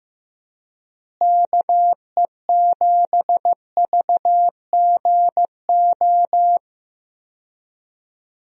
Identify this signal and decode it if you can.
Morse code, heard as a keyed tone, about 15 wpm, 700 Hz. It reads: KE7VGO
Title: USA Call Signs - 6 Characters - Repeated 3x - 15wpm